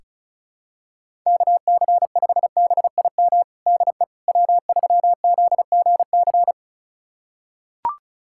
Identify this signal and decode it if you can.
Morse code, heard as a keyed tone, about 36 wpm, 690 Hz. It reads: KC5BIM DE W3ZGC
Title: USA Callsign DE Callsign 35wpm